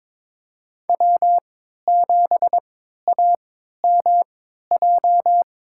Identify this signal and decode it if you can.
Morse code, heard as a keyed tone, about 22 wpm, spaced as for 12 wpm, 700 Hz. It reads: W7AMJ
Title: USA Call Signs - 5 Characters - 3x Character Spacing - 22wpm